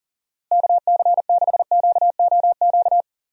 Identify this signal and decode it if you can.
Morse code, heard as a keyed tone, about 38 wpm, 700 Hz. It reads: KC6QOQ
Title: USA Call Signs - 6 Characters - Repeated 3x - 40wpm